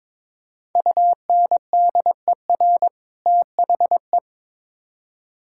Morse code, heard as a keyed tone, about 22 wpm, 700 Hz. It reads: UNDER THE
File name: Build A Phrase - 2-Word Phrases - 22wpm